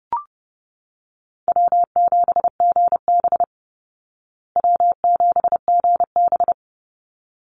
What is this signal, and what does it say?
Morse code, heard as a keyed tone, about 30 wpm, 700 Hz. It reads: W7GB W7GB
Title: Easy Call Signs Repeated 2x - 30wpm